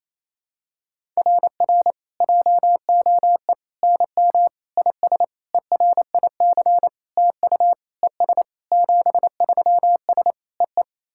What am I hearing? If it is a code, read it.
Morse code, heard as a keyed tone, about 28 wpm, 700 Hz. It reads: RR JOE NM IS ERIC TU ES 73S EE
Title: SOTA QSO Practice 28wpm